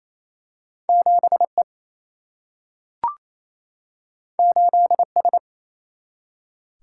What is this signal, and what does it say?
Morse code, heard as a keyed tone, about 28 wpm, 700 Hz. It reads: 7E 8S
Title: Single-Letter Single-Number 28wpm